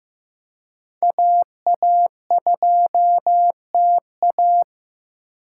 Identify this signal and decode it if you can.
Morse code, heard as a keyed tone, about 15 wpm, 700 Hz. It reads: AA2TA
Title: USA Call Signs - 5 Characters - 3x Character Spacing - 15wpm